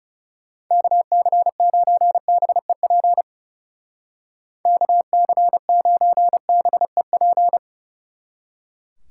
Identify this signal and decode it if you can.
Morse code, heard as a keyed tone, about 37 wpm, 700 Hz. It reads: KC9BEP KC9BEP
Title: Speed Racing - USA Call Signs 30wpm